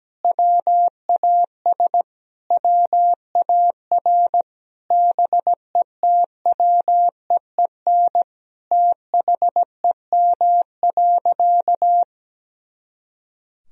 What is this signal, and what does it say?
Morse code, heard as a keyed tone, about 17 wpm, 700 Hz. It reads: WAS WAR BETWEEN THEM.
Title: Sentences from Top 400 Words and Abbreviations 17wpm